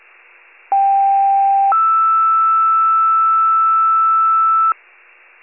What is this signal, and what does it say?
Two-tone page, heard as a one-second tone, then a three-second tone, about 790 then 1320 Hz. It is heard from 0.7 to 4.7 s.